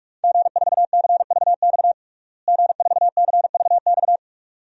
Morse code, heard as a keyed tone, about 46 wpm, 700 Hz. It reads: G4CVX G4CVX
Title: DXCC-100 Call Signs - 5 Characters - Repeated 2x - 45wpm